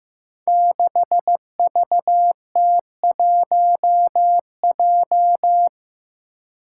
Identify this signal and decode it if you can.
Morse code, heard as a keyed tone, about 15 wpm, 700 Hz. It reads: 6VT1J
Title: Five Challenging Letter-Number Combinations 15wpm